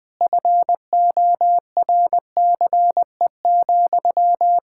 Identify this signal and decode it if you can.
Morse code, heard as a keyed tone, about 21 wpm, 700 Hz. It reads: FORCE,
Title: Sentences from Top 1000 Words - Review Part 5 - 20wpm